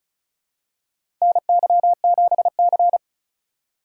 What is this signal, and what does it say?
Morse code, heard as a keyed tone, about 36 wpm, 700 Hz. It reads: NY7C
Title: USA Call Signs - 2x Character Spacing - 35wpm